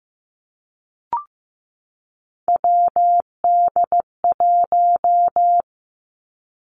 Morse code, heard as a keyed tone, about 15 wpm, 700 Hz. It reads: WD1
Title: Three Challenging Letter-Number Combinations 15wpm